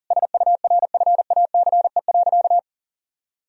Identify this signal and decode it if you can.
Morse code, heard as a keyed tone, about 37 wpm, 700 Hz. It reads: SURFACE.
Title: Sentences from Top 1000 Words - Review Part 3 - 40wpm